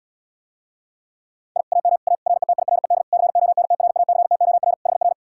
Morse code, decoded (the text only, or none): I AM A RETIRED CONTRACTOR HR